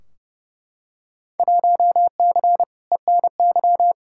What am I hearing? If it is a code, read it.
Morse code, heard as a keyed tone, about 30 wpm, 700 Hz. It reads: 1C ENY
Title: Field Day Exchanges Encore - 30wpm